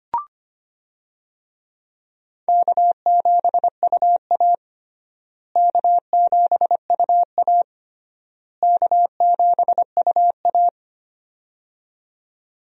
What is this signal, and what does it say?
Morse code, heard as a keyed tone, about 25 wpm, 700 Hz. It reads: K7UA K7UA K7UA
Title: USA Call Signs - 4 Characters - Repeated 3x - 25wpm